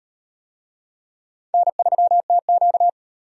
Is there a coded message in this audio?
Yes: Morse code from 1.5 to 2.9 s, heard as a keyed tone, about 37 wpm, 700 Hz. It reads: N3TQ